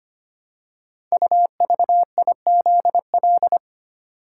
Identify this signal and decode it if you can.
Morse code, heard as a keyed tone, about 25 wpm, 700 Hz. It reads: UVIZL
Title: Code Groups - 5 Letters - 25wpm